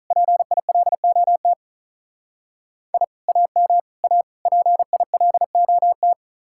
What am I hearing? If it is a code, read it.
Morse code, heard as a keyed tone, about 44 wpm, 700 Hz. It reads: PILOT I AM A PILOT
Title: Speed-Racing Common QSO Phrases - 35wpm